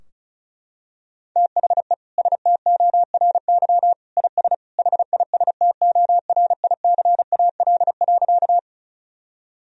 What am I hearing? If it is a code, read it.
Morse code, heard as a keyed tone, about 34 wpm, 700 Hz. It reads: THE STORY IS HISTORICAL.
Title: Sentences from Top 1000 Words - Review Part 4 - 35wpm